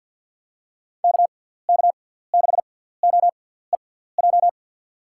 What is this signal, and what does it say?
Morse code, heard as a keyed tone, about 49 wpm, spaced as for 17 wpm, 700 Hz. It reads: KK6OEJ
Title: USA Call Signs - 6 Characters - 6x Character Spacing - 50wpm